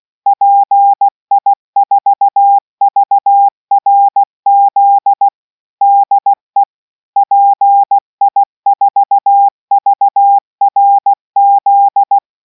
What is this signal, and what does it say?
Morse code, heard as a keyed tone, about 16 wpm, 800 Hz. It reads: PI4VRZ DE PI4VRZ